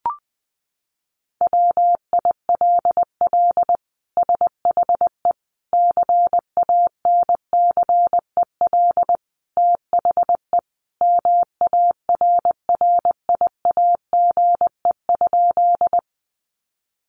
Morse code, heard as a keyed tone, about 20 wpm, 700 Hz. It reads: WILL SHE CANCEL THE MARRIAGE?
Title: Sentences from Top 1200 Words 20wpm